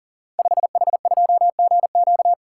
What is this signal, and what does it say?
Morse code, heard as a keyed tone, about 42 wpm, 700 Hz. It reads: 5H2GQ
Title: Code Groups - 5 Letter:Numbers - 40wpm